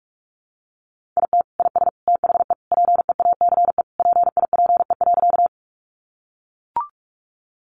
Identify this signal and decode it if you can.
Morse code, heard as a keyed tone, about 44 wpm, 700 Hz. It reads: IT IS THE PEACE PIPE.
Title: Sentences from Top 1500 Words 45wpm